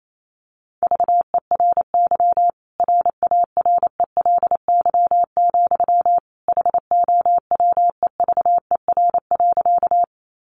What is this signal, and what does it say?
Morse code, heard as a keyed tone, about 28 wpm, 700 Hz. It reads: VERY RARELY, HOWEVER.
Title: Sentences from Top 1400 Words 28wpm